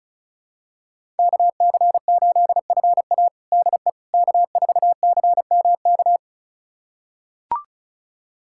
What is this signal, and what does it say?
Morse code, heard as a keyed tone, about 34 wpm, 700 Hz. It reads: KC8FA DE K4CMK
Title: USA Callsign DE Callsign - 5 Characters - 35wpm